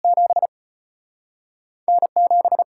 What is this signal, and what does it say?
Morse code, heard as a keyed tone, about 33 wpm, spaced as for 6 wpm, 700 Hz. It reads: 7N 7